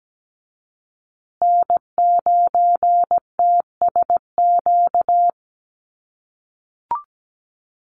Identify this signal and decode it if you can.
Morse code, heard as a keyed tone, about 17 wpm, 700 Hz. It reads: N9TSQ
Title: USA Call Signs Repeated 2x 17wpm